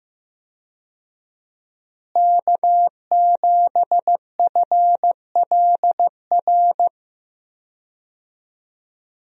Morse code, heard as a keyed tone, about 15 wpm, 700 Hz. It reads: K7FLR
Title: USA Call Signs - 5 Characters - Repeated 3x - 15wpm